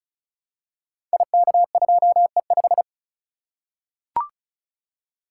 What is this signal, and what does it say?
Morse code, heard as a keyed tone, about 36 wpm, 700 Hz. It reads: IK2E5
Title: Code Groups - 5 Letter:Numbers - 35wpm